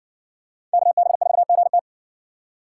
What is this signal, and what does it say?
Morse code, heard as a keyed tone, about 60 wpm, 700 Hz. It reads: KB4GT